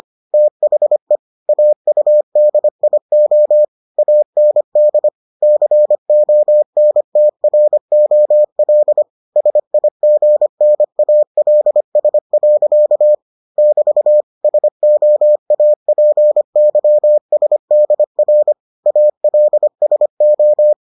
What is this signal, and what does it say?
Morse code, heard as a keyed tone, about 25 wpm, 600 Hz. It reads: THE AUDIO AND CONTROL SIGNALS. <BT> SOAPYSDR ALSO